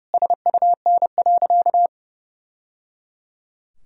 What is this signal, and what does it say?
Morse code, heard as a keyed tone, about 30 wpm, 700 Hz. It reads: SUN.
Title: Sentences from Top 1000 Words - Review Part 1 - 30wpm